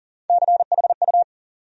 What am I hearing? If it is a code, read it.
Morse code, heard as a keyed tone, about 39 wpm, 700 Hz. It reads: CHU
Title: USA Call Sign Postfixes - 3 Letters 40wpm